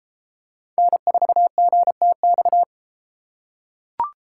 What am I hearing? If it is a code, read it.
Morse code, heard as a keyed tone, about 33 wpm, 700 Hz. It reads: N4GTX